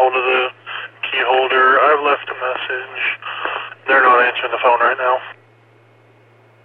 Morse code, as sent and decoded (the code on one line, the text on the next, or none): none